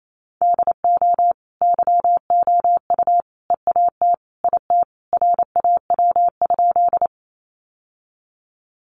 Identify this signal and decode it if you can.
Morse code, heard as a keyed tone, about 28 wpm, 700 Hz. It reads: DO YOU EAT IT RAW?
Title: Sentences from Top 1500 Words 28wpm